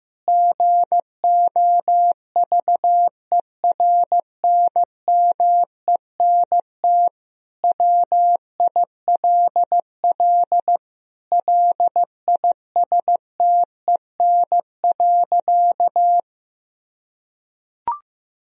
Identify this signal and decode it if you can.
Morse code, heard as a keyed tone, about 15 wpm, 700 Hz. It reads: GOVERNMENT WILL LISTEN.
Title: Sentences from Top 1000 Words - Review Part 3 - 15wpm